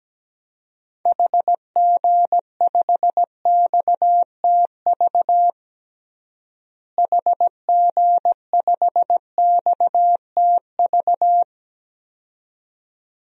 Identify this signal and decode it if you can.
Morse code, heard as a keyed tone, about 17 wpm, 700 Hz. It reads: HG5XTV HG5XTV
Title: DXCC-100 Call Signs - 6 Characters - Repeated 2x - 17wpm